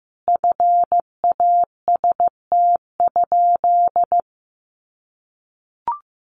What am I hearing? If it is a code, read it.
Morse code, heard as a keyed tone, about 15 wpm, 700 Hz. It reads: FAST?
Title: Sentences from Top 500 Words 15wpm